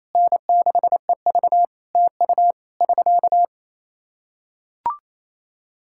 Morse code, heard as a keyed tone, about 28 wpm, 700 Hz. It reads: N6EV TU <SK>
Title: Practice QSO - Exchange 4 - Ending - 28wpm